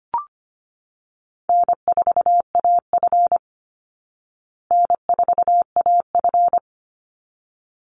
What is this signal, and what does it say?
Morse code, heard as a keyed tone, about 25 wpm, 700 Hz. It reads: N4AF N4AF
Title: Easy Call Signs Repeated 2x - 25wpm